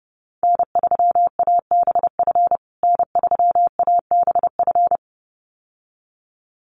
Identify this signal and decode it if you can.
Morse code, heard as a keyed tone, about 30 wpm, 700 Hz. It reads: N3ABF N3ABF
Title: USA Call Signs Repeated 2x 30wpm